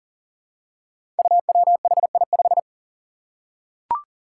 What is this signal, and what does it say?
Morse code, heard as a keyed tone, about 41 wpm, 700 Hz. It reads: UWHI5
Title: Five Challenging Letter-Number Combinations 40wpm